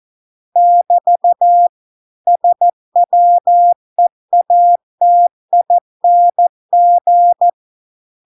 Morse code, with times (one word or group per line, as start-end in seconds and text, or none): 0.6-1.7 <BT>
2.3-7.5 SWEATING